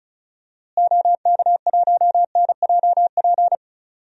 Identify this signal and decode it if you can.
Morse code, heard as a keyed tone, about 36 wpm, 700 Hz. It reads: OK1NJP
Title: DXCC-100 Call Signs - 6 Characters - Repeated 3x - 35wpm